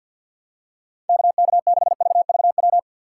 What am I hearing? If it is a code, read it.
Morse code, heard as a keyed tone, about 50 wpm, 700 Hz. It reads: KK6VVW
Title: USA Call Signs - 6 Characters - 4x Character Spacing - 50wpm